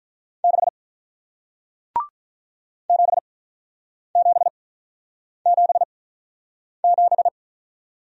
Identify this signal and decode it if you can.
Morse code, heard as a keyed tone, about 47 wpm, spaced as for 9 wpm, 700 Hz. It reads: 6 7777